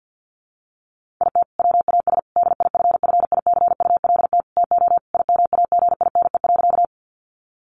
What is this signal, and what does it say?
Morse code, heard as a keyed tone, about 50 wpm, 700 Hz. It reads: IT WAS DIFFICULT TO IMAGINE.
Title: Sentences from Top 1400 Words 50wpm